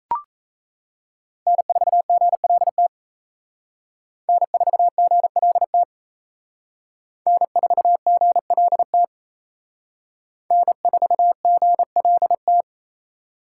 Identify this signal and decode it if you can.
Morse code, heard as a keyed tone, about 34 wpm, 700 Hz. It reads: N4GLT N4GLT N4GLT N4GLT
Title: Speed Racing - USA 1x3 Call Signs 28wpm